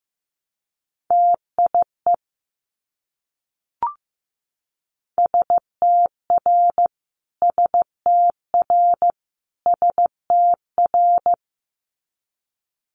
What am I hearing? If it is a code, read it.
Morse code, heard as a keyed tone, about 15 wpm, 700 Hz. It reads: TIE STR STR STR
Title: Multi-letter Phonemes Repeated 3x 15wpm